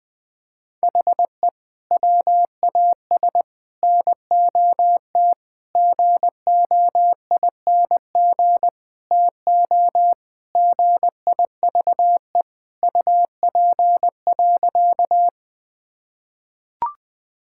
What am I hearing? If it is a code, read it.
Morse code, heard as a keyed tone, about 20 wpm, 700 Hz. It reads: HE WAS NOT GOING TO GIVE UP.
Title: Sentences with 5-Word Phrases - 20wpm